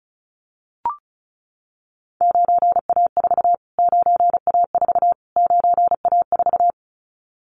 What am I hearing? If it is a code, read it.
Morse code, heard as a keyed tone, about 34 wpm, 700 Hz. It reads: 9A4 9A4 9A4
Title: Callsign Prefixes for Top 25 DXCC - Repeated 3x - 35wpm